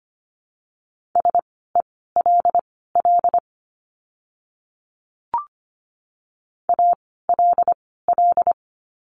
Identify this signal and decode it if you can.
Morse code, heard as a keyed tone, about 25 wpm, spaced as for 16 wpm, 700 Hz. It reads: SELL ALL
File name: course / 25:10wpm / 094 Lesson 16f - Letter c - Words 25wpm